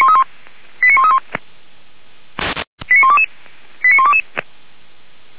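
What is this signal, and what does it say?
Five-tone selective call (ZVEI), heard as quick runs of five tones, tones of about 70 ms. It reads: 89131 89122 89122
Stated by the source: ZVEI five-tone selective call received by SDR, logged on the Feuerwehr Orlamünde fire brigade server.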